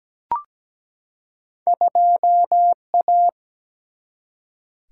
Morse code, heard as a keyed tone, about 17 wpm, 700 Hz. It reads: 2A
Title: Single-Letter Single-Number 17wpm